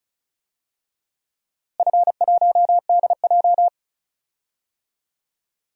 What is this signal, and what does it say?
Morse code, heard as a keyed tone, about 34 wpm, 700 Hz. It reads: F1DJ
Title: DXCC-100 Call Signs - 4 Characters - Repeated 3x - 35wpm